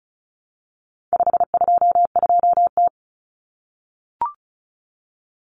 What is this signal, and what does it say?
Morse code, heard as a keyed tone, about 34 wpm, 700 Hz. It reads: 522T
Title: Common Cut Numbers - 4-Digit 35wpm